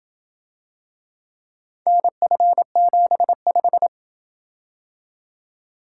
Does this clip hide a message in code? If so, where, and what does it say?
Yes: Morse code from 1.9 to 3.9 s, heard as a keyed tone, about 27 wpm, 700 Hz. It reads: NF75